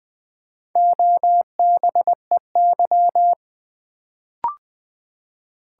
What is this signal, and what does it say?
Morse code, heard as a keyed tone, about 20 wpm, 700 Hz. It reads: OBEY
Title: Build A Word - 1 Letter At A Time - 4-Letter Words - 20wpm